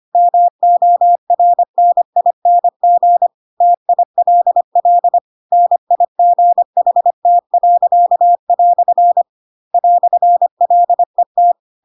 Morse code, heard as a keyed tone, about 25 wpm, 700 Hz. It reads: MORNING TILL NIGHT." "LET
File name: Pinocchio / Pinocchio ch4 25 25 ik1hge